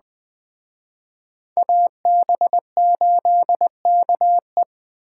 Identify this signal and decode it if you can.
Morse code, heard as a keyed tone, about 21 wpm, 700 Hz. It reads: AB8KE
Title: Speed Racing - USA 2x2 Call Signs 17wpm